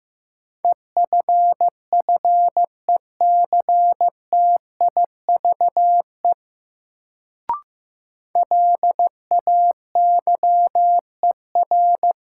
Morse code, heard as a keyed tone, about 15 wpm, 700 Hz. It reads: EFFECTIVE LAYER
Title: Sentences from Top 1600 Words - New Word Drill Alt - 15wpm